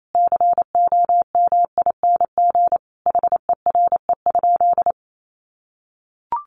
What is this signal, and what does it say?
Morse code, heard as a keyed tone, about 28 wpm, 700 Hz. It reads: COMING HERE?
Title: Sentences from Top 1300 Words 28wpm